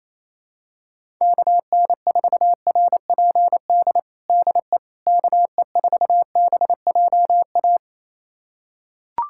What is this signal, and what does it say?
Morse code, heard as a keyed tone, about 27 wpm, 700 Hz. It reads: KN4RPD DE KE4BJA
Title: USA Callsign DE Callsign - 6 Characters - 28wpm